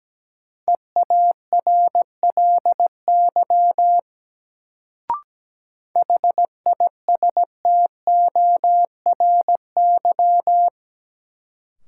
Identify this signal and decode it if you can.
Morse code, heard as a keyed tone, about 17 wpm, 700 Hz. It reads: EARLY HISTORY
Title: Sets of 3 Words - New Word Drill - 17wpm